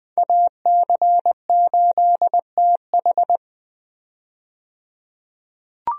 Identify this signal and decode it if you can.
Morse code, heard as a keyed tone, about 20 wpm, 700 Hz. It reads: AC8TH
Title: USA Call Signs - 5 Characters - 20wpm